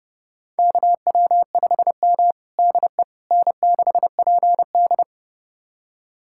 Morse code, heard as a keyed tone, about 32 wpm, 700 Hz. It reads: KW5M DE N6PD
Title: USA Callsign DE Callsign - 4 Characters - 30wpm